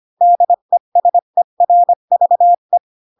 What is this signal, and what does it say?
Morse code, heard as a keyed tone, about 25 wpm, 700 Hz. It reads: DESERVE
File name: Pinocchio / Pinocchio ch3 20 25 ik1hge